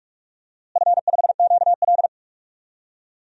Morse code, heard as a keyed tone, about 43 wpm, 700 Hz. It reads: F5QL
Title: DXCC-100 Call Signs - 4 Characters - Repeated 3x - 45wpm